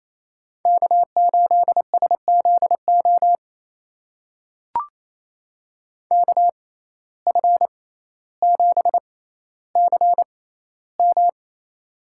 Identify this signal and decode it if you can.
Morse code, heard as a keyed tone, about 28 wpm, spaced as for 14 wpm, 700 Hz. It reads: K8SZO KF7CM
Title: USA Call Signs - 5 Characters - 6x Character Spacing - 28wpm